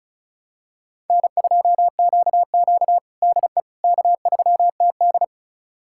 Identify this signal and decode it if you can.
Morse code, heard as a keyed tone, about 36 wpm, 700 Hz. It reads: N2QQ DE K3TD
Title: USA Callsign DE Callsign - 4 Characters - 35wpm